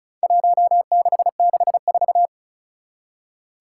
Morse code, heard as a keyed tone, about 36 wpm, 700 Hz. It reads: 1664
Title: Four Challenging Letter-Number Combinations 35wpm